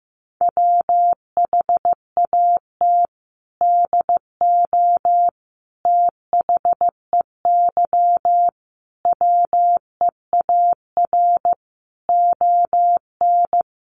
Morse code, heard as a keyed tone, about 15 wpm, 700 Hz. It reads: WHAT DO THEY WEAR ON
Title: Sentences from Top 1500 Words 15wpm